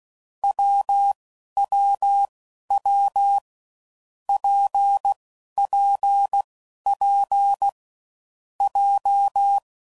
Morse code, heard as a keyed tone, about 16 wpm, 800 Hz. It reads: W W W P P P J
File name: Combinations / Difficult Combos / WPJ WPJ1